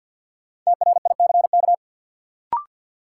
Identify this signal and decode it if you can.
Morse code, heard as a keyed tone, about 51 wpm, 700 Hz. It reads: TRICK
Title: Sentences from Top 1500 Words - New Word Drill Alt - 50wpm